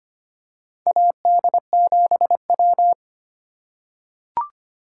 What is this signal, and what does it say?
Morse code, heard as a keyed tone, about 25 wpm, 700 Hz. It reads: AD7W
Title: USA Call Signs - 4 Characters - 2x Character Spacing - 25wpm